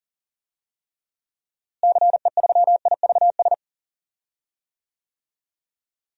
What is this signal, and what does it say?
Morse code, heard as a keyed tone, about 40 wpm, 700 Hz. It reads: CE3IVS
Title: DXCC-100 Call Signs - 6 Characters - Repeated 2x - 40wpm